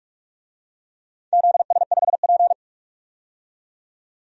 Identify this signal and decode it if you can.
Morse code, heard as a keyed tone, about 42 wpm, 700 Hz. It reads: ZS5P
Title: DXCC-100 Call Signs - 4 Characters - 45wpm